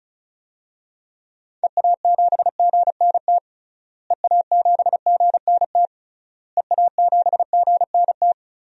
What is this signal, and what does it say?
Morse code, heard as a keyed tone, about 36 wpm, 700 Hz. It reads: EA7GNT EA7GNT EA7GNT
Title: DXCC-100 Call Signs - 6 Characters - Repeated 3x - 35wpm